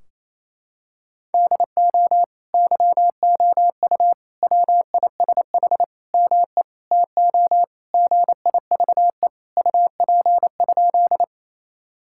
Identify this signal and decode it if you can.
Morse code, heard as a keyed tone, about 28 wpm, 700 Hz. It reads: DO YOU WISH ME TO GIVE UP?
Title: Sentences from Top 1600 Words 28wpm